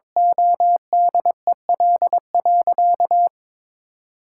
Morse code, heard as a keyed tone, about 22 wpm, 700 Hz. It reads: ODEL.
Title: Sentences from Top 1000 Words - Review Part 5 - 22wpm